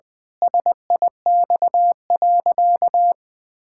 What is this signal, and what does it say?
Morse code, heard as a keyed tone, about 20 wpm, 700 Hz. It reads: SIX.
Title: Pangram Challenge v2 - 20wpm